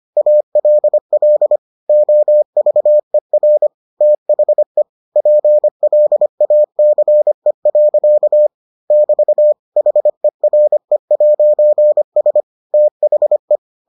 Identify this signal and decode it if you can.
Morse code, heard as a keyed tone, about 25 wpm, 600 Hz. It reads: ALL OVER THE PLACE. <BT> HERE'S THE